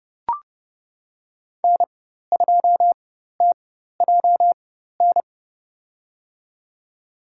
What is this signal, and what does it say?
Morse code, heard as a keyed tone, about 31 wpm, spaced as for 14 wpm, 700 Hz. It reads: N2TJN
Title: USA Call Signs - 5 Characters - 4x Character Spacing - 30wpm